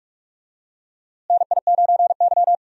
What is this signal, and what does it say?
Morse code, heard as a keyed tone, about 43 wpm, 700 Hz. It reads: NI9Y
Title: USA Call Signs - 4 Characters - Repeated 3x - 45wpm